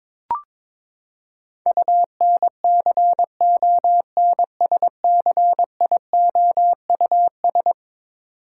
Morse code, heard as a keyed tone, about 22 wpm, 700 Hz. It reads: UNCONSCIOUS
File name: practice / 22wpm / Eleven Letter Words - 22wpm